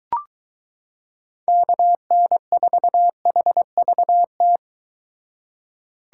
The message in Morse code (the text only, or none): KN4HVT